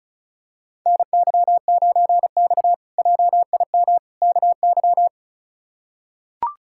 Morse code, heard as a keyed tone, about 36 wpm, 700 Hz. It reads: NY9X JIM KY
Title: K1USN SST QSO Practice - 35wpm